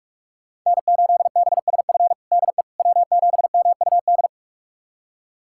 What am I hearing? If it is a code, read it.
Morse code, heard as a keyed tone, about 46 wpm, 700 Hz. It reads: N8BSF DE W7MUD
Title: USA Callsign DE Callsign - 5 Characters - 45wpm